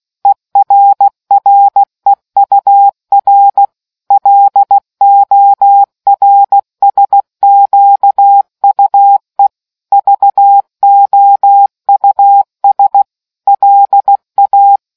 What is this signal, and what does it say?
Morse code, heard as a keyed tone, about 16 wpm, 800 Hz. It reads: ERREUR LORSQUE VOUS LA